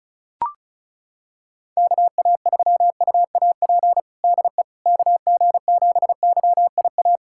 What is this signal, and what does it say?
Morse code, heard as a keyed tone, about 36 wpm, 700 Hz. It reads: KA3UAP DE KG7YIA
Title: USA Callsign DE Callsign - 6 Characters - 35wpm